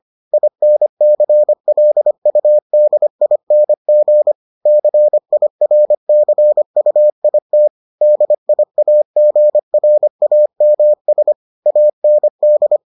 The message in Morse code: INCLUDING CIRCUIT DIAGRAMS AND

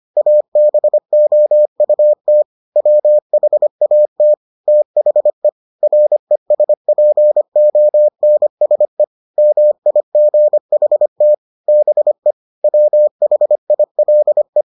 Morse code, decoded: ABOUT WHAT THE RESPONSE MIGHT BE WHILE

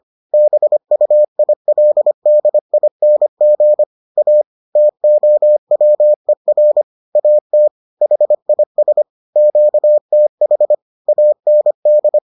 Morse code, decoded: BUILDING A TOWER AT HIS QTH AND

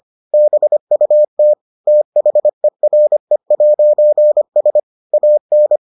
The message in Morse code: BUT THERE'S AN

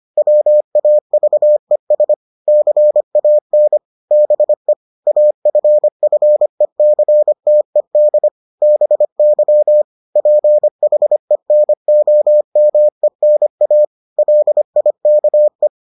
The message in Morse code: WAVES CAN BE AFFECTED BY PHENOMENA LIKE

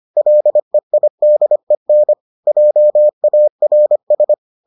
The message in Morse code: LEIDEN JARS

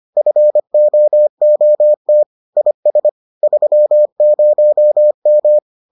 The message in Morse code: FOOT IS 30M

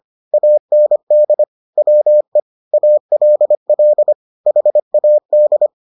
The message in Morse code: AND WE ALL HAD

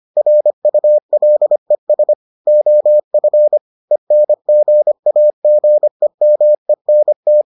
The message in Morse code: RULES OF ENGAGEMENT

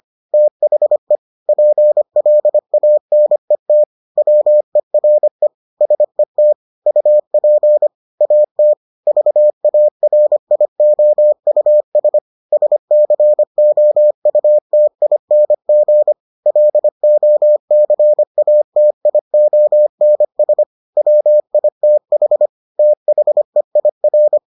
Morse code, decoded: THE PLANET WERE SET UP AT VARIOUS SCOUTING LOCATIONS WITH THEIR

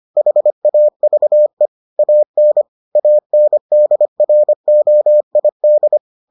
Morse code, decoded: HAVE AN ANDROID

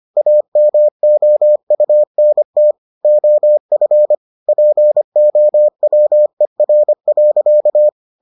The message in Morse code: AMOUNT OF POWER.

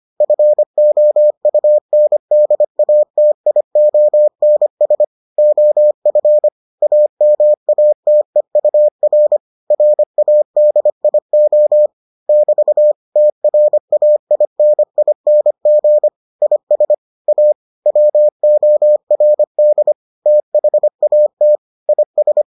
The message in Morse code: FOUNDATIONS OF AMATEUR RADIO <BT> TRAINING IS A WORD THAT IS

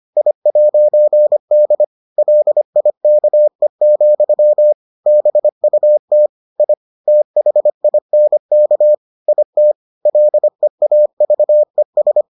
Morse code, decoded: I'D LIKE, BUT I THINK IT LEAVES

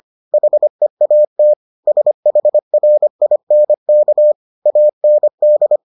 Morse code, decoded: HEAT SHRINK AND

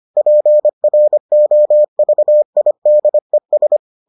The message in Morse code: PROVIDES

